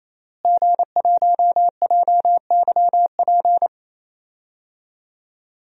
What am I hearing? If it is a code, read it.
Morse code, heard as a keyed tone, about 27 wpm, 700 Hz. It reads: G1JYP
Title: DXCC-100 Call Signs - 5 Characters - Repeated 3x - 28wpm